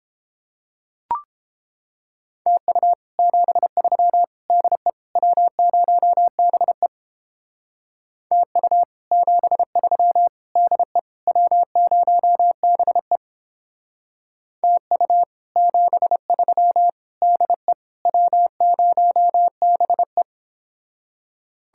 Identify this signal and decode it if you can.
Morse code, heard as a keyed tone, about 31 wpm, 700 Hz. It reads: TU 73 DE W0BE TU 73 DE W0BE TU 73 DE W0BE